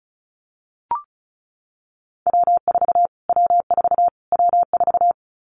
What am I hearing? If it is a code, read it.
Morse code, heard as a keyed tone, about 34 wpm, 700 Hz. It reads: W4 W4 W4
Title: Callsign Prefixes for Top 25 DXCC - Repeated 3x - 35wpm